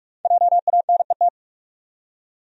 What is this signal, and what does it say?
Morse code, heard as a keyed tone, about 43 wpm, 700 Hz. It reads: JANET
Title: USA Common Names - 45wpm